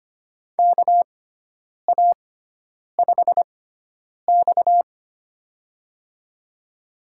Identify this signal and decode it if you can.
Morse code, heard as a keyed tone, about 25 wpm, spaced as for 9 wpm, 700 Hz. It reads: KA5X
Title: USA Call Signs - 4 Characters - 6x Character Spacing - 25wpm